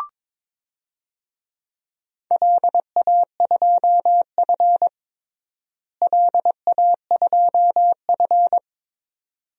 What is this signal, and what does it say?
Morse code, heard as a keyed tone, about 21 wpm, 700 Hz. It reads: LA2F LA2F
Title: DXCC-100 Call Signs - 4 Characters - Repeated 2x - 22wpm